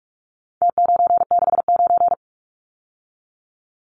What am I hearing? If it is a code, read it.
Morse code, heard as a keyed tone, about 46 wpm, 700 Hz. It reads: T969
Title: Numbers - Contest Style - 4-Digit 45wpm